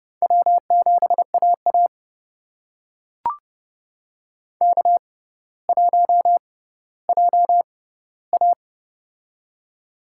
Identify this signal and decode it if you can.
Morse code, heard as a keyed tone, about 32 wpm, spaced as for 16 wpm, 700 Hz. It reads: W7AA K1JA
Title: USA Call Signs - 4 Characters - 6x Character Spacing - 30wpm